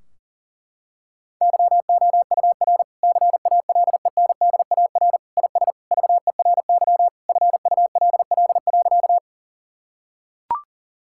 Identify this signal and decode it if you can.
Morse code, heard as a keyed tone, about 40 wpm, 700 Hz. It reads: YOUR CALENDAR IS VERY FULL.
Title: Sentences from Top 1600 Words 40wpm